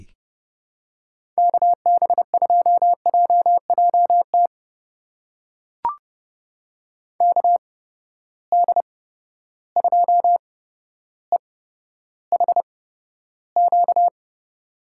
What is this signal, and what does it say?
Morse code, heard as a keyed tone, about 32 wpm, spaced as for 13 wpm, 700 Hz. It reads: KB2JJT KD2EHQ
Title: USA Call Signs - 6 Characters - 8x Character Spacing - 30wpm